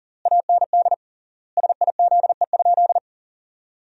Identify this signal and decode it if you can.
Morse code, heard as a keyed tone, about 37 wpm, 700 Hz. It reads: AND SIZE?
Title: 3x Word-Spacing - Sentences from Top 500 Words - 40wpm